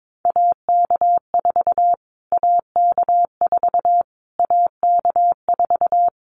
Morse code, heard as a keyed tone, about 22 wpm, 700 Hz. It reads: AK4 AK4 AK4
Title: Callsign Prefixes for Top 25 DXCC - Repeated 3x - 22wpm